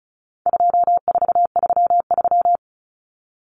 Morse code, heard as a keyed tone, about 34 wpm, 700 Hz. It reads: 2433
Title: Common Cut Numbers - 4-Digit 35wpm